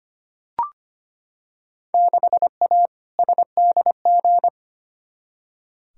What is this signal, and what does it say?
Morse code, heard as a keyed tone, about 25 wpm, 700 Hz. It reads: 6A SDG
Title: Field Day Exchanges Encore - 25wpm